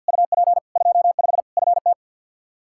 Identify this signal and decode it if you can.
Morse code, heard as a keyed tone, about 52 wpm, 700 Hz. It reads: UP 25 FT
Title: Practice QSO - Exchange 2 - Info - 50wpm